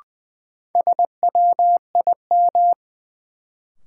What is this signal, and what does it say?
Morse code, heard as a keyed tone, about 20 wpm, 700 Hz. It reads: SWIM
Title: Sentences from Top 1000 Words - Review Part 6 - New Word Drill - 20wpm